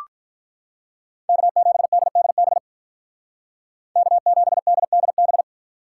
Morse code, heard as a keyed tone, about 46 wpm, 700 Hz. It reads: K7DDB K7DDB